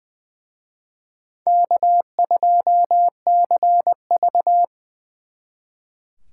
Morse code, heard as a keyed tone, about 24 wpm, 700 Hz. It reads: K2CV
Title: Speed Racing - USA 1x2 Call Signs 20wpm